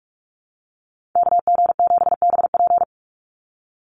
Morse code, heard as a keyed tone, about 44 wpm, 700 Hz. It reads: KG7BP
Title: USA Call Signs 45wpm